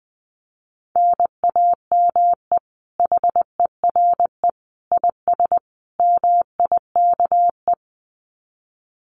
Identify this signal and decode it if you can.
Morse code, heard as a keyed tone, about 20 wpm, 700 Hz. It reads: NAME HERE IS MIKE
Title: Common QSO Phrases 20wpm